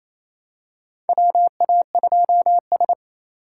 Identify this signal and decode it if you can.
Morse code, heard as a keyed tone, about 28 wpm, 700 Hz. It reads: WA2S
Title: USA Call Signs - 4 Characters - 4x Character Spacing - 28wpm